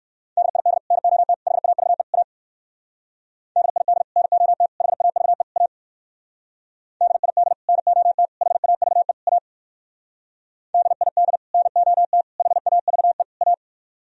Morse code, heard as a keyed tone, about 62 wpm, 700 Hz. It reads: DID NOT HAVE A DID NOT HAVE A DID NOT HAVE A DID NOT HAVE A